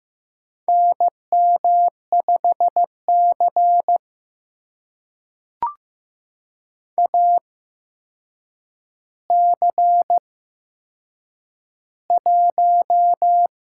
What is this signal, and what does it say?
Morse code, heard as a keyed tone, about 15 wpm, spaced as for 7 wpm, 700 Hz. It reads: NM5C AC1
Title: USA Call Signs - 4 Characters - 8x Character Spacing - 15wpm